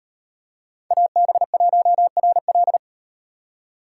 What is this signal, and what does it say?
Morse code, heard as a keyed tone, about 38 wpm, 700 Hz. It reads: AB1RL